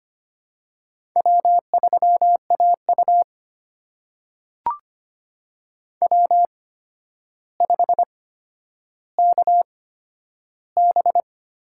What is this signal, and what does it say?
Morse code, heard as a keyed tone, about 25 wpm, spaced as for 11 wpm, 700 Hz. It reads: W3AU W5KB
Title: USA Call Signs - 4 Characters - 8x Character Spacing - 25wpm